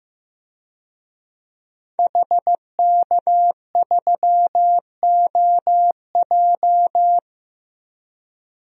Morse code, heard as a keyed tone, about 15 wpm, 700 Hz. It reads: HK3OJ
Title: DXCC-100 Call Signs - 5 Characters - 15wpm